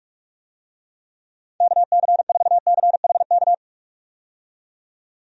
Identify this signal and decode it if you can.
Morse code, heard as a keyed tone, about 46 wpm, 700 Hz. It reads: KC4CHK
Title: USA Call Signs - 6 Characters - 45wpm